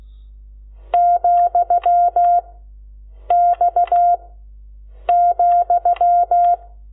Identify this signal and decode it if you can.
Morse code, heard as a keyed tone, about 16 wpm, 700 Hz. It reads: , X ,
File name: Lecciones / CLASE Nº12